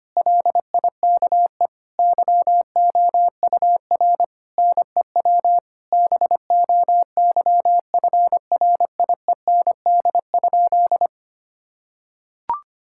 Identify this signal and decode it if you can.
Morse code, heard as a keyed tone, about 25 wpm, 700 Hz. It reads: LIKE YOUR NEW BOYFRIEND?
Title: Sentences from Top 1000 Words - Review Part 5 - 25wpm